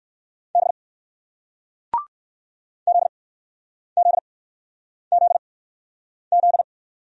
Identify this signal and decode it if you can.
Morse code, heard as a keyed tone, about 60 wpm, spaced as for 9 wpm, 700 Hz. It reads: B ZZZZ